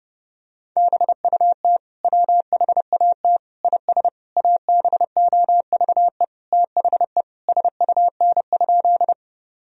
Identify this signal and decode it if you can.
Morse code, heard as a keyed tone, about 30 wpm, 700 Hz. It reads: BUT WHAT IS ABOVE THE SUN?
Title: Sentences from Top 1000 Words - Review Part 2 - 30wpm